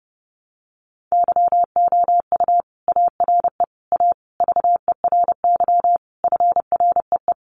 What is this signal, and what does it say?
Morse code, heard as a keyed tone, about 30 wpm, 700 Hz. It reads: YOU ARE A VERY FREE
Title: Sentences from Top 1000 Words 30wpm